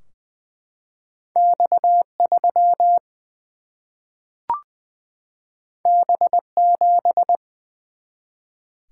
Morse code, heard as a keyed tone, about 20 wpm, 700 Hz. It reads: X 3 B 7
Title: Single-Letter Single-Number 20wpm